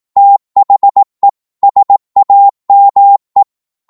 Morse code, heard as a keyed tone, about 18 wpm, 800 Hz. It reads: THE SAME